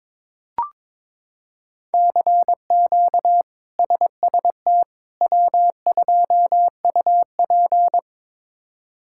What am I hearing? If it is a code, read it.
Morse code, heard as a keyed tone, about 22 wpm, 700 Hz. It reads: CQ SST W2UP
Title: K1USN SST QSO Practice - 22wpm